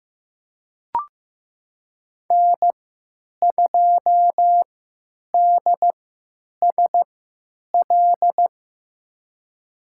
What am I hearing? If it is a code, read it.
Morse code, heard as a keyed tone, about 15 wpm, spaced as for 8 wpm, 700 Hz. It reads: N2DSL
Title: USA Call Signs - 5 Characters - 3x Character Spacing - 15wpm